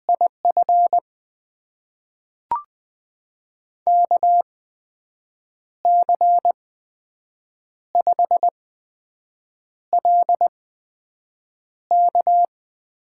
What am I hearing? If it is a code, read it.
Morse code, heard as a keyed tone, about 20 wpm, spaced as for 6 wpm, 700 Hz. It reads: IF KC5LK
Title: USA Call Signs - 5 Characters - 8x Character Spacing - 20wpm